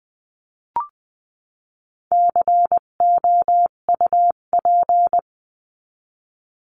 Morse code, heard as a keyed tone, about 20 wpm, 700 Hz. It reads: COUP
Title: 258 Lesson 42c - Words 16-20wpm